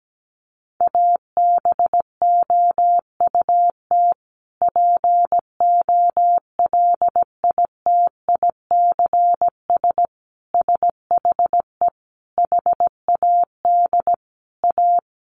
Morse code, heard as a keyed tone, about 17 wpm, 700 Hz. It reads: ABOUT POLITICS SHE HAD A
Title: Sentences from Top 1000 Words 17wpm